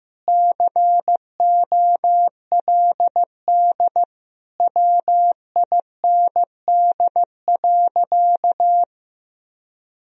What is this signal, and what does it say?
Morse code, heard as a keyed tone, about 15 wpm, 700 Hz. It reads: COLD WIND.
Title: Sentences from Top 1000 Words - Review Part 2 - 15wpm